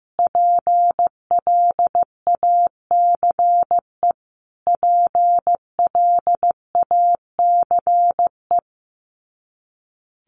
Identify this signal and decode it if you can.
Morse code, heard as a keyed tone, about 15 wpm, 700 Hz. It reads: PLACE PLACE
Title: Top 300 Words Repeated 15wpm